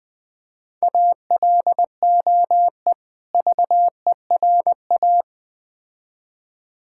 Morse code, heard as a keyed tone, about 21 wpm, 700 Hz. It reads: ALOE VERA
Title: Difficult Words with Hints - 20wpm